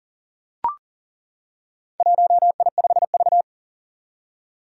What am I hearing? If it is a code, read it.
Morse code, heard as a keyed tone, about 40 wpm, 700 Hz. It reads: 1I5V
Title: Four Challenging Letter-Number Combinations 40wpm